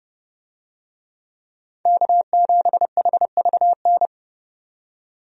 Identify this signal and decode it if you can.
Morse code, heard as a keyed tone, about 30 wpm, 700 Hz. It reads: K7HVN
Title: USA Call Signs - 5 Characters - 30wpm